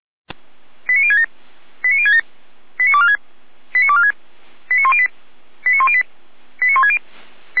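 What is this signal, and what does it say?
Five-tone selective call (ZVEI), heard as quick runs of five tones, tones of about 70 ms. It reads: 89067 89067 89246 89246 89108 89108 89160
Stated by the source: ZVEI five-tone selective call received by SDR.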